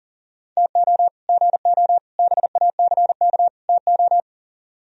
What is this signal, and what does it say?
Morse code, heard as a keyed tone, about 51 wpm, 700 Hz. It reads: TO GO BACK TO